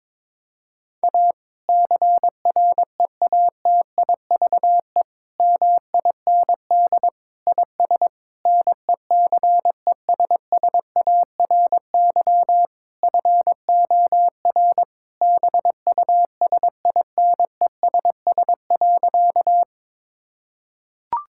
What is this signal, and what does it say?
Morse code, heard as a keyed tone, about 22 wpm, 700 Hz. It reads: A CREATIVE MIND IS NECESSARY FOR BUSINESS.
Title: Sentences from Top 1000 Words - Review Part 5 - 22wpm